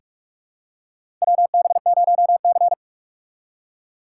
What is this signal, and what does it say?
Morse code, heard as a keyed tone, about 46 wpm, 700 Hz. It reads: WB0C
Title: USA Call Signs - 4 Characters - 4x Character Spacing - 45wpm